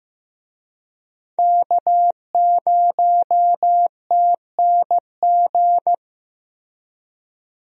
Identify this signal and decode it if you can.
Morse code, heard as a keyed tone, about 15 wpm, 700 Hz. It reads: K0TNG
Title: USA Call Signs - 5 Characters - Repeated 2x - 15wpm